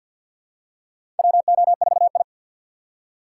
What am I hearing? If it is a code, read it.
Morse code, heard as a keyed tone, about 51 wpm, 700 Hz. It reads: WO4I
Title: USA Call Signs - 4 Characters - 2x Character Spacing - 50wpm